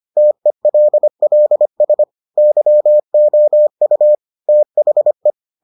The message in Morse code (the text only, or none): TELLS YOU THE